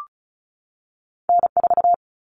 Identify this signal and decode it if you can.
Morse code, heard as a keyed tone, about 36 wpm, 700 Hz. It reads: N4
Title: Common Cut Numbers - 2-Digit 35wpm